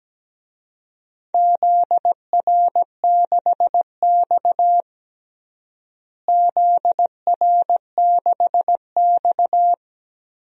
Morse code, heard as a keyed tone, about 17 wpm, 700 Hz. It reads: ZR6X ZR6X
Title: DXCC-100 Call Signs - 4 Characters - Repeated 3x - 17wpm